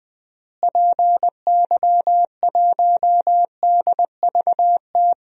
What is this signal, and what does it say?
Morse code, heard as a keyed tone, about 19 wpm, 700 Hz. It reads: PY1DVT
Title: DXCC-100 Call Signs - 6 Characters - Repeated 3x - 20wpm